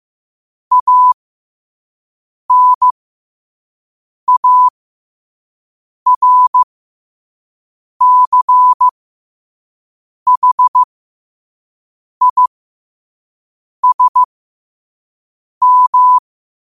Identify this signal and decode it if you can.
Morse code, heard as a keyed tone, about 15 wpm, spaced as for 5 wpm, 1000 Hz. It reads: ANARCHISM
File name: qotdfiles / 5wpm